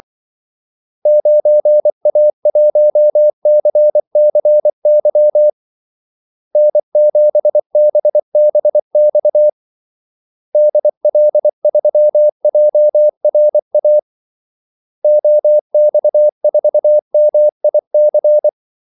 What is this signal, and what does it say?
Morse code, heard as a keyed tone, about 24 wpm, 600 Hz. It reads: 9A1CCY N7BBX DL3JRA OX4MIC
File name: Callsigns-24wpm